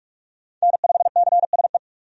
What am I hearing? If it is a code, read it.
Morse code, heard as a keyed tone, about 41 wpm, 700 Hz. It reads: N5CSE
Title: USA Call Signs - 5 Characters - 45wpm